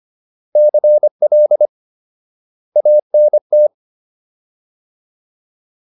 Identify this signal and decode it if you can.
Morse code, heard as a keyed tone, about 25 wpm, 600 Hz. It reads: CL ANT